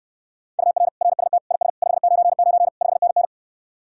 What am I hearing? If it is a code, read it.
Morse code, heard as a keyed tone, about 69 wpm, 700 Hz. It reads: UR RST IS 599 5NN